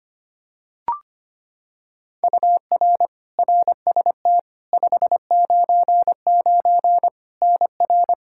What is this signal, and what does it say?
Morse code, heard as a keyed tone, about 25 wpm, 700 Hz. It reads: UR RST 599 NR
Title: Practice QSO - Exchange 1 - Beginning - 25wpm